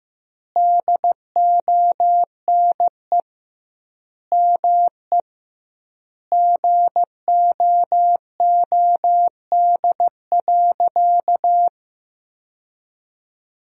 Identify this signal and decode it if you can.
Morse code, heard as a keyed tone, about 15 wpm, 700 Hz. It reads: DONE ME GOOD.
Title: 2x Word-Spacing - Sentences from Top 500 Words - 15wpm